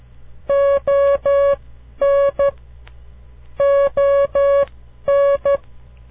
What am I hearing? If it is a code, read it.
Morse code, heard as a keyed tone, about 12 wpm, spaced as for 10 wpm, 560 Hz. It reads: ON ON
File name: CW Learning Programs / farnsworth / unit 2